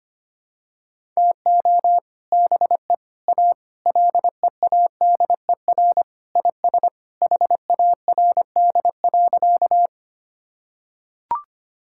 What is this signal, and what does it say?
Morse code, heard as a keyed tone, about 25 wpm, 700 Hz. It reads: TO BE A LEADER IS HARD.
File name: practice / 25wpm / Sentences with 3-Word Phrases - 25wpm